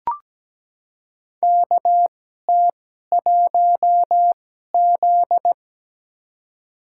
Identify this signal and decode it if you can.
Morse code, heard as a keyed tone, about 17 wpm, spaced as for 12 wpm, 700 Hz. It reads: KT1Z
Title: USA Call Signs - 4 Characters - 2x Character Spacing - 17wpm